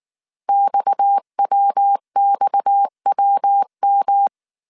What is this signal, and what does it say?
Morse code, heard as a keyed tone, about 19 wpm, 800 Hz. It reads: XWXWM